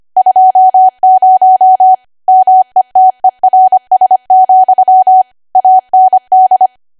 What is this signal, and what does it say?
Morse code, heard as a keyed tone, about 25 wpm, 750 Hz. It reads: 20 METERS, AND
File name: Archive / 25 WPM / 25WPM